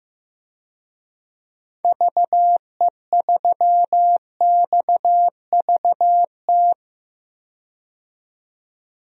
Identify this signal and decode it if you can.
Morse code, heard as a keyed tone, about 15 wpm, 700 Hz. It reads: VE3XVT
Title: DXCC-100 Call Signs - 6 Characters - 15wpm